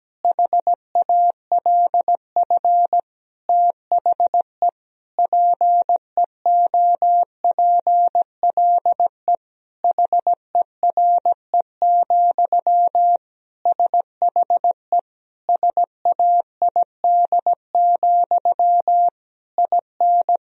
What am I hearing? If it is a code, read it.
Morse code, heard as a keyed tone, about 17 wpm, 700 Hz. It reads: HALF THE PEOPLE HERE, SHE SAID, IN
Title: Sentences from Top 1000 Words - Review Part 1 - 17wpm